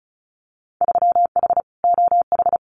Morse code, heard as a keyed tone, about 36 wpm, 700 Hz. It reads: 3H OH
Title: Winter Field Day Exchanges Encore 35wpm